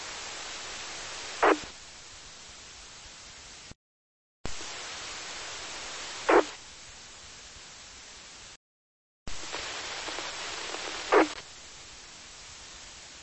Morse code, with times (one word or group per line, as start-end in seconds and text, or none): none